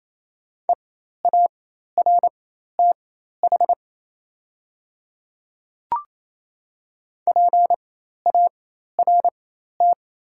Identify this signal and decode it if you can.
Morse code, heard as a keyed tone, about 27 wpm, spaced as for 13 wpm, 690 Hz. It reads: EARTH PART
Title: 4x Character Spacing - Top 200 Words - 28wpm